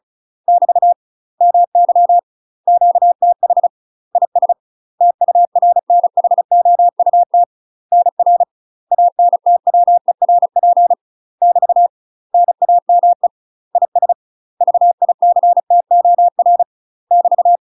Morse code, heard as a keyed tone, about 36 wpm, 700 Hz. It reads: <BT> MY QTH IS TURNHOUT NR ANTWERP <BT> NAME IS VICTOR <BT>
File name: UK MASTER QSOs TRAINING / 35WPM / UK MASTER QSOs 0033